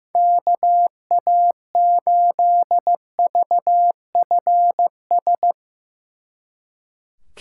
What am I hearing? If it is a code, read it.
Morse code, heard as a keyed tone, about 16 wpm, 700 Hz. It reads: KA8VFS
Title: Speed Racing - USA 2x3 Call Signs 15wpm